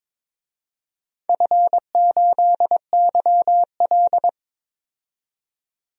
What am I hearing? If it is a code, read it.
Morse code, heard as a keyed tone, about 21 wpm, 700 Hz. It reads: F8YL
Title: DXCC-100 Call Signs - 4 Characters - 22wpm